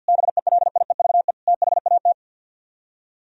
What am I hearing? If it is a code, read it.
Morse code, heard as a keyed tone, about 64 wpm, 700 Hz. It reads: BELIEVE THAT